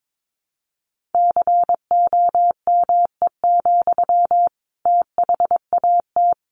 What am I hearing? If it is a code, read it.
Morse code, heard as a keyed tone, about 22 wpm, 700 Hz. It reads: COME, THAT
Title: Sentences from Top 700 Words 22wpm